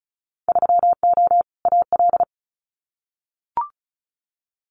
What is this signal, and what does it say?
Morse code, heard as a keyed tone, about 36 wpm, 690 Hz. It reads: 3O AL
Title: Winter Field Day Exchanges 35wpm